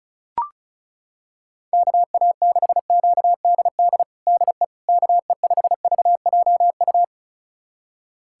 Morse code, heard as a keyed tone, about 36 wpm, 700 Hz. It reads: KA6QDD DE KE5VJU
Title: USA Callsign DE Callsign 35wpm